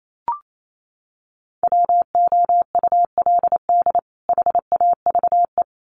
Morse code, heard as a keyed tone, about 28 wpm, 700 Hz. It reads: WOULD HAVE
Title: Sets of 2 Words 28wpm